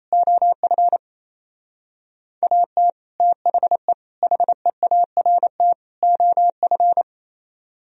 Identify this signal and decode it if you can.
Morse code, heard as a keyed tone, about 35 wpm, 700 Hz. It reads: OF AT THE HEART OF
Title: Speed-Racing - Sets of 4 Words 28wpm